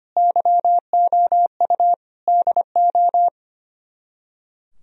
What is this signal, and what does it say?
Morse code, heard as a keyed tone, about 32 wpm, 700 Hz. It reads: YOU DO